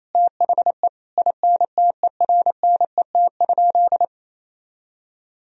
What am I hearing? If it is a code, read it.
Morse code, heard as a keyed tone, about 28 wpm, 700 Hz. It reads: THE INTERNET?
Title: Sentences from Top 1000 Words - Review Part 4 - 28wpm